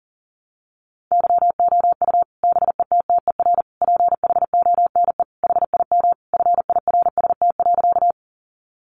Morse code, heard as a keyed tone, about 40 wpm, 700 Hz. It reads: YOU BETTER PHONE HIM FIRST.
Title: Sentences from Top 1100 Words 40wpm